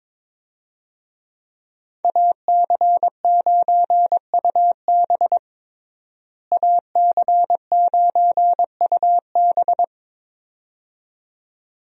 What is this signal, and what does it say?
Morse code, heard as a keyed tone, about 22 wpm, 700 Hz. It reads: AC9UB AC9UB
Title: USA Call Signs - 5 Characters - Repeated 2x - 22wpm